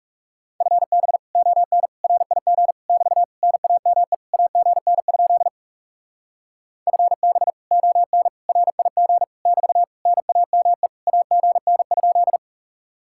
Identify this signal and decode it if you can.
Morse code, heard as a keyed tone, about 37 wpm, 700 Hz. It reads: FB ON RIG <BT> NAME AGN? FB ON RIG <BT> NAME AGN?
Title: Speed-Racing Common QSO Phrases Encore - 30wpm